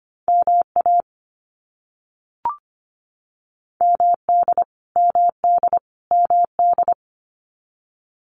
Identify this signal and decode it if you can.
Morse code, heard as a keyed tone, about 25 wpm, 700 Hz. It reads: MA MD MD MD
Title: US States Repeated 3x 25wpm